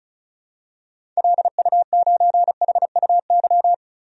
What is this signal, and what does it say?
Morse code, heard as a keyed tone, about 34 wpm, 700 Hz. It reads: LU9HUY
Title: DXCC-100 Call Signs - 6 Characters - 35wpm